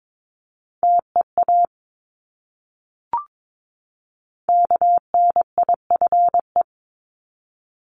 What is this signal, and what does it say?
Morse code, heard as a keyed tone, about 22 wpm, 700 Hz. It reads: TEA KNIFE
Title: Top 1751-2000 Words 22wpm